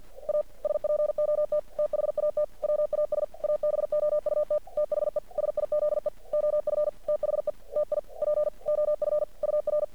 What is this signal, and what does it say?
Morse code, heard as a keyed tone, about 46 wpm, 600 Hz. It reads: A SPOT THAT WAS ABOUT THE SIZE OF THE TIP OF UR
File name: archived / CRSnet